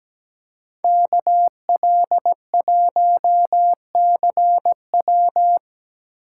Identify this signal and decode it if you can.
Morse code, heard as a keyed tone, about 20 wpm, 700 Hz. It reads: KL1CW
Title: Speed Racing - USA Call Signs 17wpm